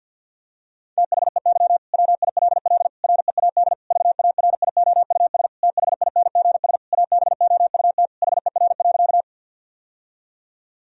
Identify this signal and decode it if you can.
Morse code, heard as a keyed tone, about 50 wpm, 700 Hz. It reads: THEY WILL READ VARIOUS THINGS ABOUT HER.
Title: Sentences from Top 1000 Words - Review Part 4 - 50wpm